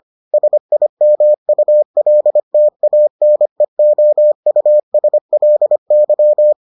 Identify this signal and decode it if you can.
Morse code, heard as a keyed tone, about 25 wpm, 600 Hz. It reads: SIMULTANEOUSLY